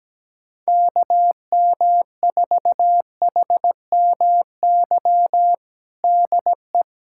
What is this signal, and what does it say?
Morse code, heard as a keyed tone, about 17 wpm, 700 Hz. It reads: KM4HMY DE
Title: USA Callsign DE Callsign - 6 Characters - 17wpm